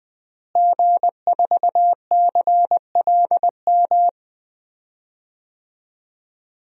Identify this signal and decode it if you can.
Morse code, heard as a keyed tone, about 20 wpm, 700 Hz. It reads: G4CLM
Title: DXCC-100 Call Signs - 5 Characters - Repeated 3x - 20wpm